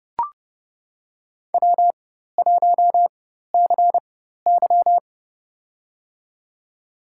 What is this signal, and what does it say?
Morse code, heard as a keyed tone, about 32 wpm, spaced as for 14 wpm, 700 Hz. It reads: W1CY
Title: USA Call Signs - 4 Characters - 4x Character Spacing - 30wpm